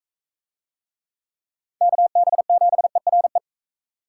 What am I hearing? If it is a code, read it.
Morse code, heard as a keyed tone, about 41 wpm, 700 Hz. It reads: KB7ERE